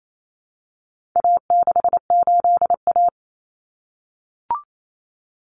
Morse code, heard as a keyed tone, about 28 wpm, 700 Hz. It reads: A68A
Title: Common Cut Numbers - 4-Digit 28wpm